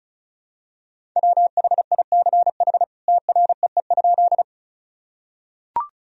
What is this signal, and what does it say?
Morse code, heard as a keyed tone, about 34 wpm, 700 Hz. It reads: WHICH TREE?
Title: Sentences from Top 1000 Words - Review Part 1 - 35wpm